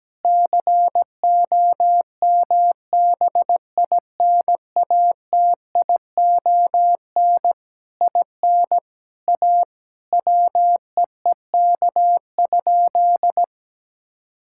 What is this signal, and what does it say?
Morse code, heard as a keyed tone, about 17 wpm, 700 Hz. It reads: COMBINATION IN A WEEK?
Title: Sentences from Top 1000 Words - Review Part 6 - 17wpm